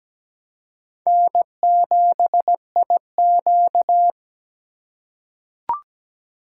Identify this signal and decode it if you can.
Morse code, heard as a keyed tone, about 17 wpm, 700 Hz. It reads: N7IQ
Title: USA Call Signs - 4 Characters - 3x Character Spacing - 17wpm